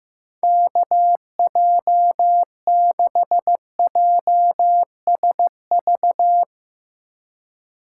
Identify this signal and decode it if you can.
Morse code, heard as a keyed tone, about 16 wpm, 700 Hz. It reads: KJ6JSV
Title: Speed Racing - USA 2x3 Call Signs 15wpm